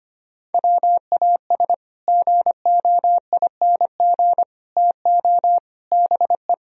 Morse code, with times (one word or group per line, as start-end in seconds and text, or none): 0.5-1.7 WAS
2.1-4.4 GOING
4.8-5.6 TO
5.9-6.5 BE